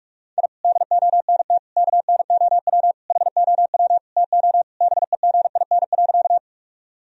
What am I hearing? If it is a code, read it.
Morse code, heard as a keyed tone, about 46 wpm, 700 Hz. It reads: I DONT KNOW HOW TO BEGIN.
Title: Sentences from Top 1000 Words - Review Part 1 - 45wpm